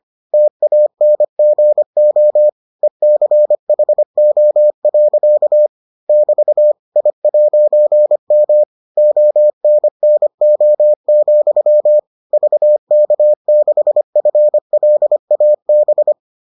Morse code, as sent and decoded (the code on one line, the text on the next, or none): - .- -. --. --- / . -.-. .... --- .-.-.- / -...- / .. .----. -- / --- -. -. --- --..-- / ...- -.- -.... ..-. .-.. .- -...
TANGO ECHO. <BT> I'M ONNO, VK6FLAB